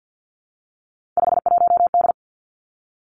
Morse code, heard as a keyed tone, about 50 wpm, 700 Hz. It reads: 51D
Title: Three Challenging Letter-Number Combinations 50wpm